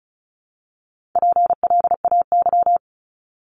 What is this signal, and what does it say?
Morse code, heard as a keyed tone, about 36 wpm, 700 Hz. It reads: PLAY
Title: Multi-letter Phonemes Repeated 3x 35wpm